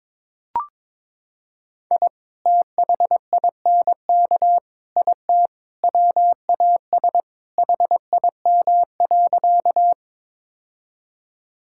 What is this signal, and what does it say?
Morse code, heard as a keyed tone, about 22 wpm, 700 Hz. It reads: I THINK IT WAS HIM.
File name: practice / 22wpm / Sentences with 4-Word Phrases - 22wpm